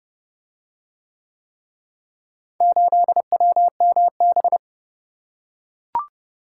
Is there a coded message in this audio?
Yes: Morse code from 2.6 to 4.6 s, heard as a keyed tone, about 27 wpm, 700 Hz. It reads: 8WMB